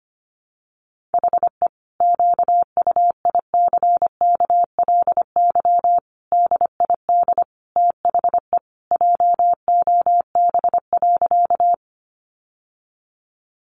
Morse code, heard as a keyed tone, about 25 wpm, 700 Hz. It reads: HE QUICKLY DID THE JOB.
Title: Sentences from Top 700 Words 25wpm